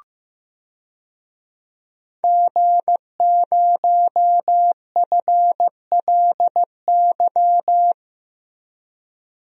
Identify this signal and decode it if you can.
Morse code, heard as a keyed tone, about 15 wpm, 700 Hz. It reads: G0FLY
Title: DXCC-100 Call Signs - 5 Characters - Repeated 3x - 15wpm